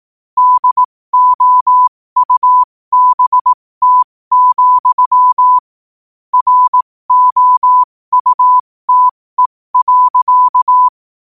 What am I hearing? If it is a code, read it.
Morse code, heard as a keyed tone, about 18 wpm, spaced as for 15 wpm, 1000 Hz. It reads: DOUBT, ROUTE.
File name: qotdfiles / 15wpm